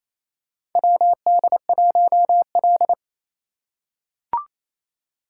Morse code, heard as a keyed tone, about 28 wpm, 700 Hz. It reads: WD1L
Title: USA Call Signs - 4 Characters - 2x Character Spacing - 28wpm